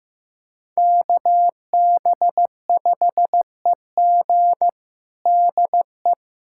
Morse code, heard as a keyed tone, about 16 wpm, 700 Hz. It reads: KB5EG DE
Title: USA Callsign DE Callsign - 5 Characters - 15wpm